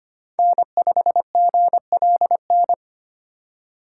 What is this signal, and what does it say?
Morse code, heard as a keyed tone, about 25 wpm, 700 Hz. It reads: N5GLN
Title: USA Call Signs - 5 Characters - 4x Character Spacing - 25wpm